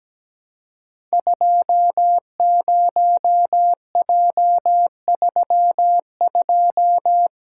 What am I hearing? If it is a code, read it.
Morse code, heard as a keyed tone, about 17 wpm, 700 Hz. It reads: 20J32
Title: Code Groups - 5 Letter:Numbers - 17wpm